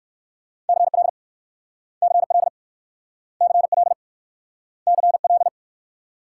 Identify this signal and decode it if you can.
Morse code, heard as a keyed tone, about 60 wpm, 700 Hz. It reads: CL CL CL CL